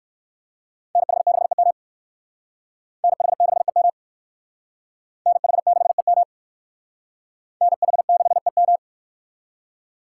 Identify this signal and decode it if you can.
Morse code, heard as a keyed tone, about 59 wpm, 700 Hz. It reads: NH6EM NH6EM NH6EM NH6EM